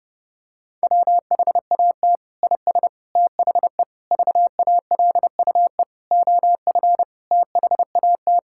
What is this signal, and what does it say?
Morse code, heard as a keyed tone, about 30 wpm, 700 Hz. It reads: WHAT IS THE VALUE OF THAT
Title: Sentences from Top 1000 Words - Review Part 6 - 30wpm